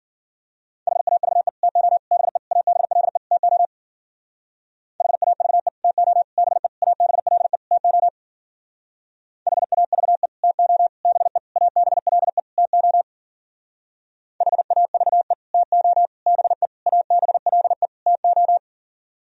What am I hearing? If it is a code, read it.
Morse code, heard as a keyed tone, about 51 wpm, 700 Hz. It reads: HAVE TO BE ABLE TO HAVE TO BE ABLE TO HAVE TO BE ABLE TO HAVE TO BE ABLE TO